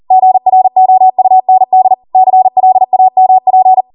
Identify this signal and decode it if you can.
Morse code, heard as a keyed tone, about 40 wpm, 750 Hz. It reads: GROUND CLAMP